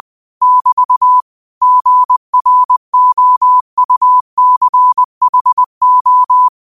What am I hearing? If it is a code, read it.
Morse code, heard as a keyed tone, about 20 wpm, 1000 Hz. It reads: <BT> GROUCHO
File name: qotdfiles / 20wpm